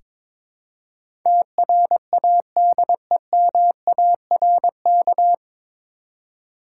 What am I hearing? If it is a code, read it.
Morse code, heard as a keyed tone, about 22 wpm, 700 Hz. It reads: TRADEMARK
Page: Morse Code Ninja practice 22wpm